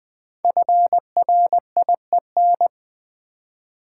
Morse code, heard as a keyed tone, about 20 wpm, 700 Hz. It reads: FRIEN
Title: Build A Word - 5 Letters At A Time - 10-Letter Words - 20wpm